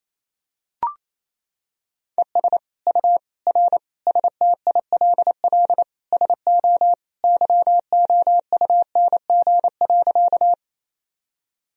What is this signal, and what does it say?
Morse code, heard as a keyed tone, about 28 wpm, 700 Hz. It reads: ES U R STILL SO YOUNG.
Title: Sentences from Top 400 Words and Abbreviations 28wpm